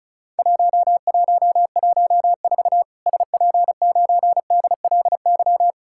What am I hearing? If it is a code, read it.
Morse code, heard as a keyed tone, about 36 wpm, 700 Hz. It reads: 1114 SP9DLY
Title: Worked All Europe DX Contest - QTC Exchange - 35wpm